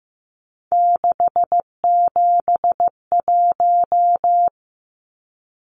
Morse code, heard as a keyed tone, about 15 wpm, 700 Hz. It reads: 671
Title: Numbers - 3-Digit 15wpm